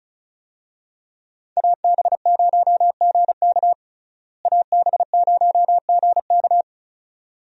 Morse code, heard as a keyed tone, about 34 wpm, 700 Hz. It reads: AB0GK AB0GK
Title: USA Call Signs - 5 Characters - Repeated 2x - 35wpm